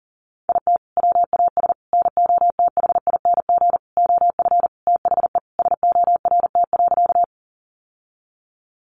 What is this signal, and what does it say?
Morse code, heard as a keyed tone, about 40 wpm, 700 Hz. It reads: IT WAS NOTHING OF THE SORT.
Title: Sentences from Top 1300 Words 40wpm